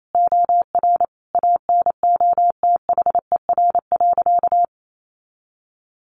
Morse code, heard as a keyed tone, about 28 wpm, 700 Hz. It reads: OR ANOTHER.
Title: Sentences from Top 1400 Words 28wpm